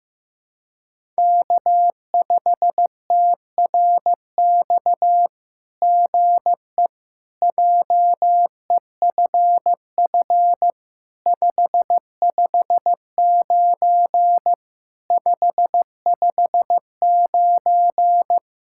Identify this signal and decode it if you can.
Morse code, heard as a keyed tone, about 15 wpm, 700 Hz. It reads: K5TRX GE JEFF 559 559
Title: SOTA QSO Practice 15wpm